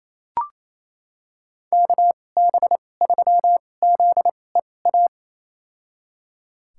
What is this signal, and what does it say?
Morse code, heard as a keyed tone, about 28 wpm, spaced as for 20 wpm, 700 Hz. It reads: KB3ZEA
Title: USA Call Signs - 6 Characters - 2x Character Spacing - 28wpm